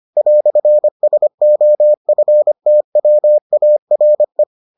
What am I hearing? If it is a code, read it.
Morse code, heard as a keyed tone, about 25 wpm, 600 Hz. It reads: "SOFTWARE